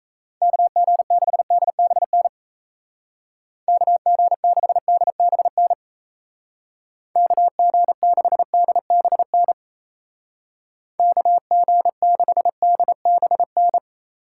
Morse code, heard as a keyed tone, about 36 wpm, 700 Hz. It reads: KG6DBN KG6DBN KG6DBN KG6DBN